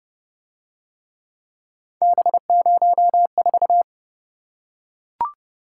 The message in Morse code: B04

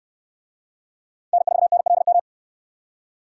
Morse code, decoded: N4NLM

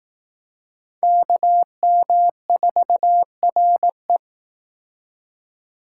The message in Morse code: KM4RE

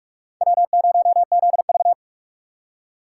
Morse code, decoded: W0ZV

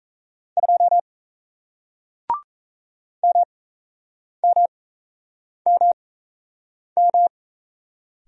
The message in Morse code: 2 MMMM

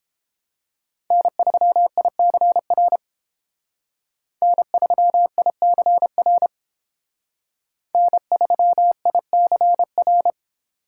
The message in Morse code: N3ICR N3ICR N3ICR

